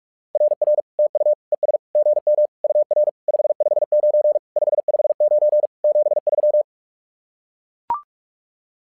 Morse code, heard as a keyed tone, about 44 wpm, 600 Hz. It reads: RR TU ES GM UR 559 559 73
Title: SOTA QSO Practice 45wpm